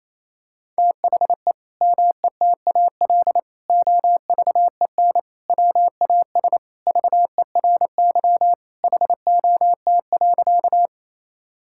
Morse code, heard as a keyed tone, about 28 wpm, 700 Hz. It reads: THE METAL OVEN WAS VERY HOT.
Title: Sentences from Top 1000 Words - Review Part 6 - 28wpm